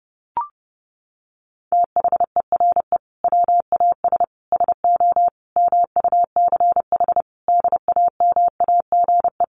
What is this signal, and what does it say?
Morse code, heard as a keyed tone, about 30 wpm, 700 Hz. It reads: THERE WAS SO MUCH DAMAGE
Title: Sentences from Top 1100 Words 30wpm